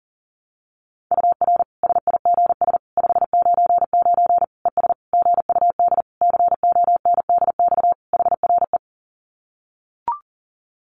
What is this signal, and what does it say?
Morse code, heard as a keyed tone, about 40 wpm, 700 Hz. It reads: UR SIGS 599 ES GUD CONDX HRE
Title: Common QSO Phrases 40wpm